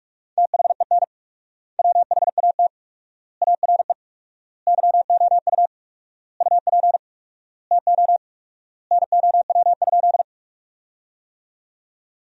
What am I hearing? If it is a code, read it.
Morse code, heard as a keyed tone, about 46 wpm, 700 Hz. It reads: THEN WHAT ARE YOU UP TO NOW?
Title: 4x Word-Spacing - Sentences from Top 100 Words - 45wpm